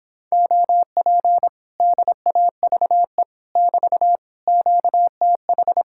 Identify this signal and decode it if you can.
Morse code, heard as a keyed tone, about 26 wpm, 700 Hz. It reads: OP DAVE <BT> QTH